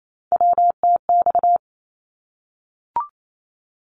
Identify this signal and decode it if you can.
Morse code, heard as a keyed tone, about 28 wpm, 700 Hz. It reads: WTX
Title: ARRL Sections 28wpm